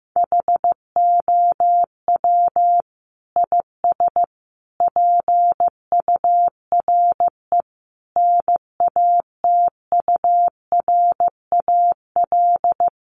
Morse code, heard as a keyed tone, about 15 wpm, 700 Hz. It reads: HOW IS PURE NATURAL
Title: Sentences from Top 1500 Words 15wpm